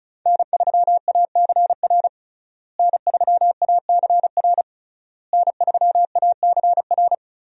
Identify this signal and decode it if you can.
Morse code, heard as a keyed tone, about 34 wpm, 700 Hz. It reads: N3ACR N3ACR N3ACR
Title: USA Call Signs - 5 Characters - Repeated 3x - 35wpm